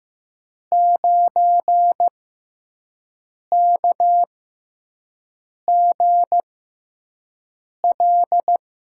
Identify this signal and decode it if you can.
Morse code, heard as a keyed tone, about 15 wpm, spaced as for 5 wpm, 700 Hz. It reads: 9KGL
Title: USA Call Signs - 5 Characters - 6x Character Spacing - 15wpm